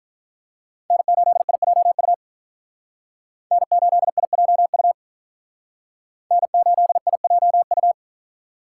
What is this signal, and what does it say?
Morse code, heard as a keyed tone, about 45 wpm, 700 Hz. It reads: N8IJU N8IJU N8IJU